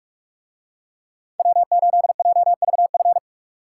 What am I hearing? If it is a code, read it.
Morse code, heard as a keyed tone, about 44 wpm, 700 Hz. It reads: W8JVF